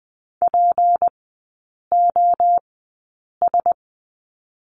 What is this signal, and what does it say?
Morse code, heard as a keyed tone, about 20 wpm, spaced as for 8 wpm, 700 Hz. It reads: POS
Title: 178 Lesson 29e - Slash - Three Characters 20wpm